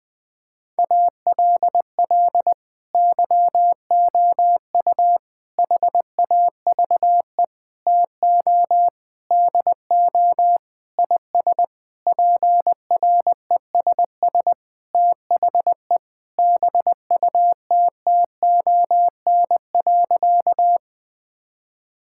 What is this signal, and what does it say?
Morse code, heard as a keyed tone, about 20 wpm, 700 Hz. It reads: ALL YOU HAVE TO DO IS PRESS THE BUTTON.
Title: Sentences with 6-Word Phrases - 20wpm